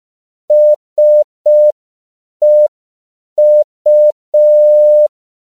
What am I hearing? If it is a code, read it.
Morse code, heard as a keyed tone, about 5 wpm, 600 Hz. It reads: SEU
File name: 5PPM